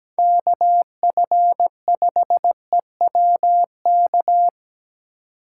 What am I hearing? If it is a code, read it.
Morse code, heard as a keyed tone, about 20 wpm, 700 Hz. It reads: KF5EWK
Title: Speed Racing - USA 2x3 Call Signs 17wpm